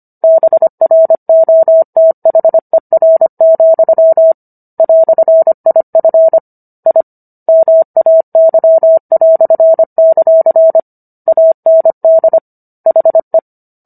Morse code, heard as a keyed tone, about 25 wpm, 650 Hz. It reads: BROTHER, "IF I MAY"; AND HE